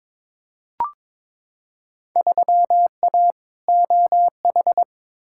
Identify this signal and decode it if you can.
Morse code, heard as a keyed tone, about 23 wpm, 700 Hz. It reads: 3A OH
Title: Field Day Exchanges - 22wpm